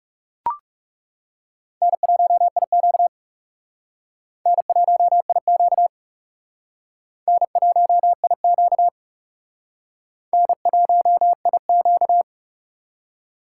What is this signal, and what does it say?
Morse code, heard as a keyed tone, about 35 wpm, 700 Hz. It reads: N1IQ N1IQ N1IQ N1IQ
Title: Speed Racing - USA Call Signs 30wpm